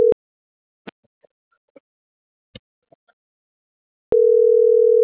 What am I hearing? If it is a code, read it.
Ringback tone (North America), 440 + 480 Hz.
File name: CENTRAL SAGOT / out isai.hernandez